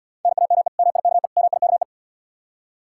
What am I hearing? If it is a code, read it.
Morse code, heard as a keyed tone, about 74 wpm, 700 Hz. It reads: NAME MIKE MIKE